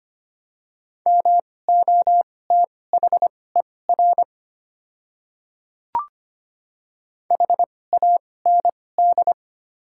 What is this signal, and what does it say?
Morse code, heard as a keyed tone, about 25 wpm, spaced as for 18 wpm, 700 Hz. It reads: MOTHER HAND
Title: 2x Character Spacing - Top 200 Words - 25wpm